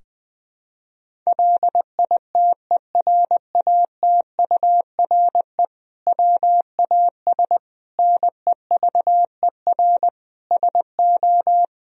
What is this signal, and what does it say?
Morse code, heard as a keyed tone, about 21 wpm, 700 Hz. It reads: LITERATURE WAS NEVER SO
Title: Sentences from Top 1000 Words - Review Part 3 - 20wpm